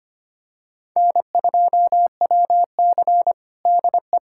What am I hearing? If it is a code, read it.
Morse code, heard as a keyed tone, about 25 wpm, 700 Hz. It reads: N2WC DE
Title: USA Callsign DE Callsign - 4 Characters - 25wpm